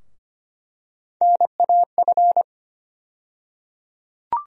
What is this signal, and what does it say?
Morse code, heard as a keyed tone, about 25 wpm, 700 Hz. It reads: NAF
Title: USA Call Sign Postfixes - 3 Letters 25wpm